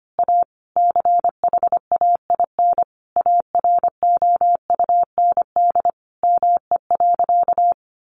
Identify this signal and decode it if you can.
Morse code, heard as a keyed tone, about 25 wpm, 700 Hz. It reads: A CHAIN AROUND ME.
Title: Sentences from Top 1500 Words 25wpm